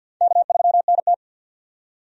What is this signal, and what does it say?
Morse code, heard as a keyed tone, about 51 wpm, 700 Hz. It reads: K3NT
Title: USA Call Signs - 4 Characters - 50wpm